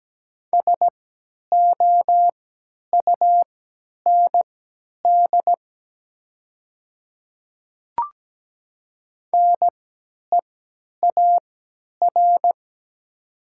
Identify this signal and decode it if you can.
Morse code, heard as a keyed tone, about 17 wpm, spaced as for 10 wpm, 700 Hz. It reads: SOUND NEAR
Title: 3x Character Spacing - Top 200 Words - 17wpm